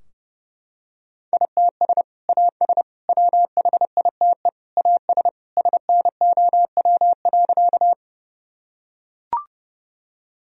Morse code, heard as a keyed tone, about 30 wpm, 700 Hz. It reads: ITS AS WHITE AS SNOW.
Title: Sentences from Top 1000 Words - Review Part 3 - 30wpm